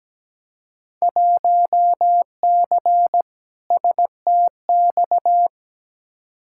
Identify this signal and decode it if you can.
Morse code, heard as a keyed tone, about 17 wpm, 700 Hz. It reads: 1C STX
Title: Field Day Exchanges Encore - 17wpm